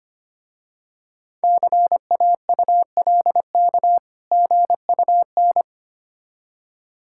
Morse code, heard as a keyed tone, about 25 wpm, 700 Hz. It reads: CAULK GUN
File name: practice / 25wpm / Difficult Words with Hints - 25wpm